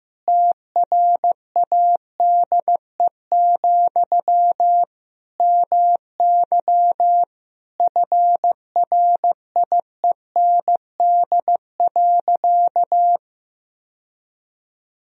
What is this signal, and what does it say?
Morse code, heard as a keyed tone, about 15 wpm, 700 Hz. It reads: TRADE, MY FRIEND.
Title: Sentences from Top 1000 Words - Review Part 4 - 15wpm